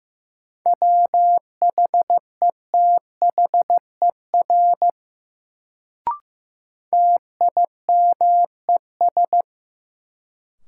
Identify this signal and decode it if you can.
Morse code, heard as a keyed tone, about 15 wpm, 700 Hz. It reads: WHETHER TIMES
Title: Sets of 3 Words - New Word Drill - 15wpm